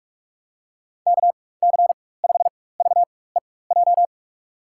Morse code, heard as a keyed tone, about 44 wpm, spaced as for 21 wpm, 700 Hz. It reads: KC5VEJ
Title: USA Call Signs - 4x Character Spacing - 45wpm